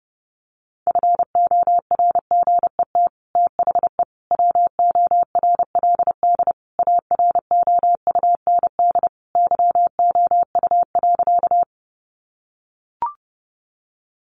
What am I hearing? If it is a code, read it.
Morse code, heard as a keyed tone, about 30 wpm, 700 Hz. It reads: FORGET THE WORLD AROUND YOU.
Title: Sentences from Top 1200 Words 30wpm